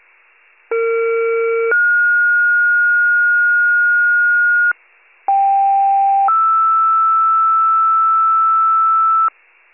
Two-tone page, heard as a one-second tone, then a three-second tone, about 460 then 1430 Hz, about 790 then 1320 Hz. It is heard from 0.7 to 9.3 s.